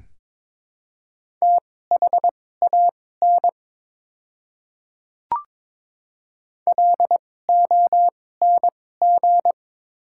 Morse code, heard as a keyed tone, about 22 wpm, spaced as for 16 wpm, 700 Hz. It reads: THAN LONG
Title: 2x Character Spacing - Top 100 Words - 22wpm